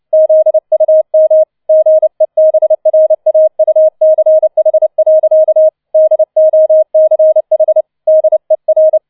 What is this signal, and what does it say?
Morse code, heard as a keyed tone, about 29 wpm, 630 Hz. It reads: ZUM GEBRAUCH. DOCH DER
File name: 006 Sechster Streich 30wpm